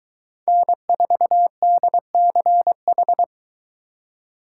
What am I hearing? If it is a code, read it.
Morse code, heard as a keyed tone, about 22 wpm, 700 Hz. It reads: N4DCH